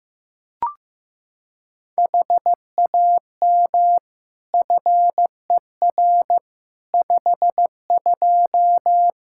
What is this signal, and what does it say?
Morse code, heard as a keyed tone, about 15 wpm, 700 Hz. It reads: HAM FER 52
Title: Practice QSO - Exchange 3 - More Info - 15wpm